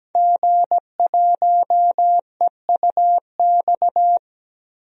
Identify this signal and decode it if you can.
Morse code, heard as a keyed tone, about 17 wpm, 700 Hz. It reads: G1EUX
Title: DXCC-100 Call Signs - 5 Characters - Repeated 2x - 17wpm